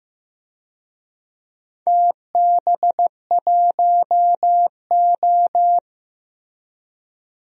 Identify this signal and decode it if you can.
Morse code, heard as a keyed tone, about 15 wpm, 700 Hz. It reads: TB1O
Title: DXCC-100 Call Signs - 4 Characters - Repeated 3x - 15wpm